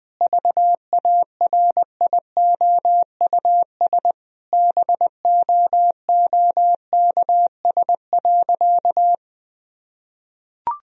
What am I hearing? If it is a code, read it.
Morse code, heard as a keyed tone, about 21 wpm, 700 Hz. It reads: VARIOUS BOOKS.
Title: Sentences from Top 1000 Words - Review Part 4 - 20wpm